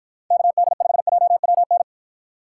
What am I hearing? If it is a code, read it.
Morse code, heard as a keyed tone, about 50 wpm, 700 Hz. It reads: KD5JWN